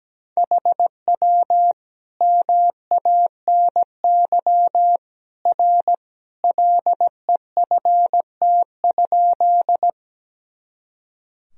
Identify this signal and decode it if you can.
Morse code, heard as a keyed tone, about 17 wpm, 700 Hz. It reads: HW MANY R LEFT?
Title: Sentences from Top 300 Words and Abbreviations 17wpm